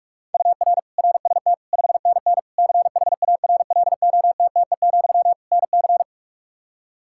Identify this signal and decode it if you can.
Morse code, heard as a keyed tone, about 46 wpm, 700 Hz. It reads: UR RST 5NN CHARLOTTE, NC
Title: K1USN SST QSO Practice 45wpm